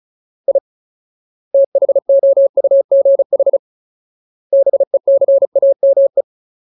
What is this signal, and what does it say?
Morse code, heard as a keyed tone, about 34 wpm, 550 Hz. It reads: I THOUGH BECAME
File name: Ultimate Word List / 35WPM Ultimate